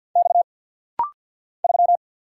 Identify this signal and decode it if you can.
Morse code, heard as a keyed tone, about 48 wpm, 690 Hz. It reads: X 3
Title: Single Letter-Number - Rapid-Fire 50wpm